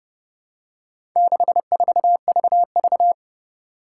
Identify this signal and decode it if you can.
Morse code, heard as a keyed tone, about 30 wpm, 700 Hz. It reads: 64VV
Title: Four Challenging Letter-Number Combinations 30wpm